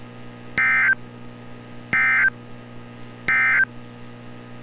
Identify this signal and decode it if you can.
Emergency Alert System (SAME) tones, heard as harsh buzzing bursts. It is heard from 0.5 to 3.7 s.
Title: NOAA Weather Radio - Altoona, PA